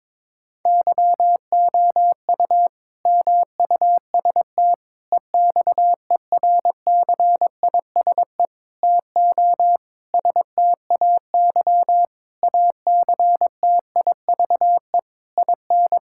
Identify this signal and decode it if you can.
Morse code, heard as a keyed tone, about 22 wpm, 700 Hz. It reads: YOU MUST EXERCISE TO STAY ACTIVE IN
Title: Sentences from Top 1000 Words - Review Part 6 - 22wpm